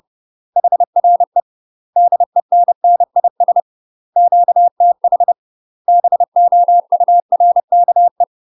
Morse code, heard as a keyed tone, about 30 wpm, 700 Hz. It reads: HRE DENNIS QTH BOURKE
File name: UK MASTER QSOs TRAINING / 30WPM / UK MASTER QSOs 0000